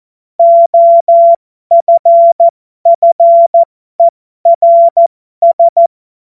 Morse code, heard as a keyed tone, about 14 wpm, spaced as for 12 wpm, 680 Hz. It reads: OFFERS
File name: Morsekurs / Witze